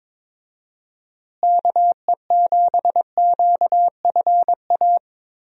Morse code, heard as a keyed tone, about 22 wpm, 700 Hz. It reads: KE7QFA
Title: USA Call Signs - 6 Characters - Repeated 2x - 22wpm